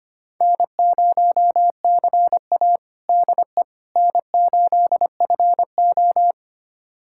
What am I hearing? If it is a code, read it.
Morse code, heard as a keyed tone, about 25 wpm, 700 Hz. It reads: N0CA DE N8FO
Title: USA Callsign DE Callsign - 4 Characters - 25wpm